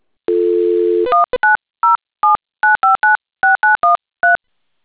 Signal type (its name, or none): DTMF